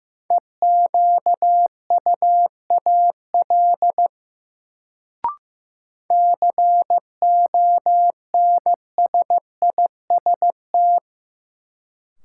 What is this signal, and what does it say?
Morse code, heard as a keyed tone, about 15 wpm, 700 Hz. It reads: EQUAL CONSIST
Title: Sentences from Top 1400 Words - New Word Drill Alt - 15wpm